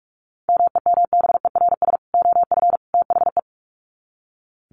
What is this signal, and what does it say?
Morse code, heard as a keyed tone, about 44 wpm, 700 Hz. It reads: MEMBERS OF THE
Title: Sets of 3 Words 45wpm